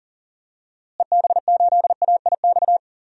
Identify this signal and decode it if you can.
Morse code, heard as a keyed tone, about 37 wpm, 700 Hz. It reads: EB8AIX
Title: DXCC-100 Call Signs - 6 Characters - Repeated 3x - 40wpm